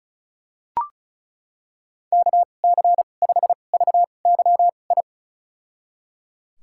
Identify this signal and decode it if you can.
Morse code, heard as a keyed tone, about 36 wpm, spaced as for 26 wpm, 690 Hz. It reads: KC5VYI
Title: USA Call Signs - 6 Characters - 2x Character Spacing - 35wpm